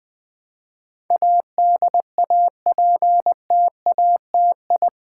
Morse code, heard as a keyed tone, about 20 wpm, 700 Hz. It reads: ADAPTATI
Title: Build A Word - 4 Letters At A Time - 10-Letter Words - 20wpm